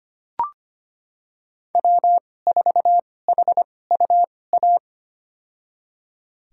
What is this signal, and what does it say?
Morse code, heard as a keyed tone, about 25 wpm, spaced as for 18 wpm, 700 Hz. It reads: W4HUA
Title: USA Call Signs - 5 Characters - 2x Character Spacing - 25wpm